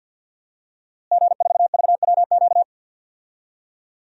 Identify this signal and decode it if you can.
Morse code, heard as a keyed tone, about 46 wpm, 700 Hz. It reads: G4VWQ
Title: DXCC-100 Call Signs - 5 Characters - Repeated 2x - 50wpm